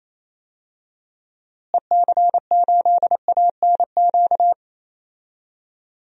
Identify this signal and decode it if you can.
Morse code, heard as a keyed tone, about 27 wpm, 700 Hz. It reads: EC8ANQ
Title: DXCC-100 Call Signs - 6 Characters - Repeated 2x - 28wpm